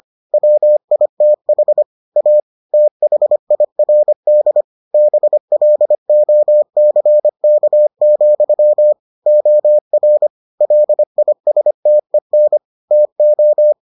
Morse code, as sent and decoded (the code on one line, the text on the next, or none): .-- .. - .... / .- / - .... .. .-. -.. / -... .-.. --- -.-. -.- --..-- / --- .-. / .-.. .. ... - . -. / - ---
WITH A THIRD BLOCK, OR LISTEN TO